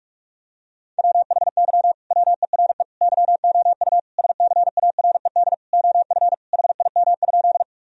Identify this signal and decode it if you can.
Morse code, heard as a keyed tone, about 41 wpm, 700 Hz. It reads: WHY WERE YOU SCARED OF HIM?
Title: Sentences from Top 1000 Words - Review Part 4 - 45wpm